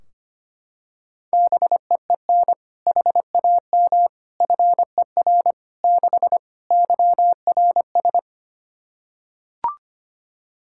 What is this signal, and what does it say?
Morse code, heard as a keyed tone, about 25 wpm, 700 Hz. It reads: BEEN HAM FER 6 YRS
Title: Practice QSO - Exchange 3 - More Info - 25wpm